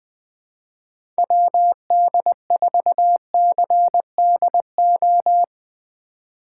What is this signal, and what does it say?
Morse code, heard as a keyed tone, about 20 wpm, 700 Hz. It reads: WD4CDO